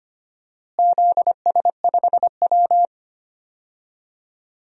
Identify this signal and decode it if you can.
Morse code, heard as a keyed tone, about 25 wpm, 700 Hz. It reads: ZS5W
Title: DXCC-100 Call Signs - 4 Characters - Repeated 2x - 25wpm